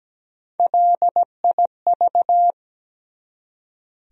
Morse code, heard as a keyed tone, about 17 wpm, 700 Hz. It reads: LIV
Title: USA Common Names - Max 3 Letters - 17wpm